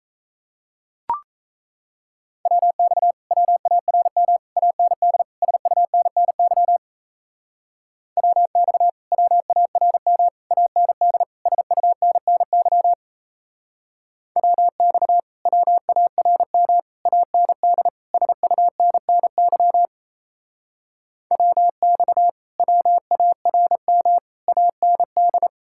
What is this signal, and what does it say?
Morse code, heard as a keyed tone, about 36 wpm, 700 Hz. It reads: WX WARM AND SUNNY WX WARM AND SUNNY WX WARM AND SUNNY WX WARM AND